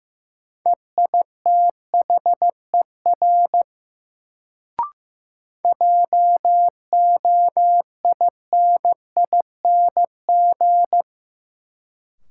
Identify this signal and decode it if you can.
Morse code, heard as a keyed tone, about 15 wpm, 700 Hz. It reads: EITHER JOINING
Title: Sets of 5 Words - New Word Drill - 15wpm